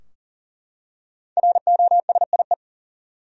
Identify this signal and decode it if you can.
Morse code, heard as a keyed tone, about 39 wpm, 700 Hz. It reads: ROSIE
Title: USA Common Names - Max 5 Letters - 40wpm